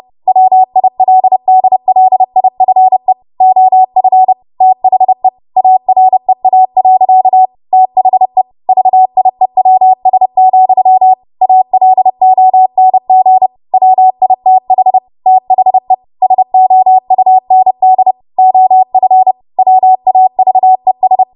Morse code, heard as a keyed tone, about 30 wpm, 750 Hz. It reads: WILDLIFE OF THE AREA. THE VIEWS, ALONG WITH THE SOUND OF WAVES
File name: Morse / 30WPM